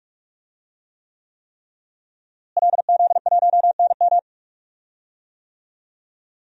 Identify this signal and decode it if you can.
Morse code, heard as a keyed tone, about 44 wpm, 700 Hz. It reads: LZ1NM
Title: DXCC-100 Call Signs - 5 Characters - Repeated 3x - 45wpm